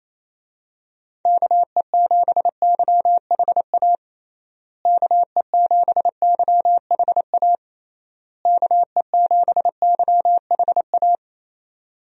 Morse code, heard as a keyed tone, about 28 wpm, 700 Hz. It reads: KE7YHA KE7YHA KE7YHA
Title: USA Call Signs - 6 Characters - Repeated 3x - 28wpm